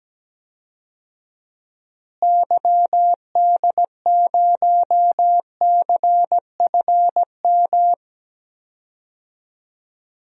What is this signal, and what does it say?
Morse code, heard as a keyed tone, about 17 wpm, 700 Hz. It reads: YD0CFM
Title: DXCC-100 Call Signs - 6 Characters - Repeated 3x - 17wpm